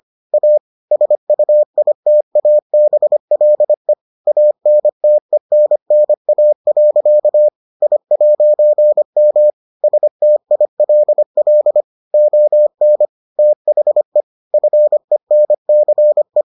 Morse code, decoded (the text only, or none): A SUITABLE ANTENNA. I'M STILL ON THE FENCE